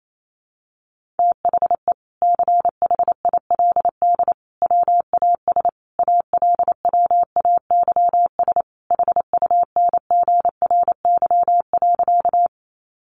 Morse code, heard as a keyed tone, about 28 wpm, 700 Hz. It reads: THE CHILD WAS ALWAYS HUNGRY.
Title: Sentences from Top 1400 Words 28wpm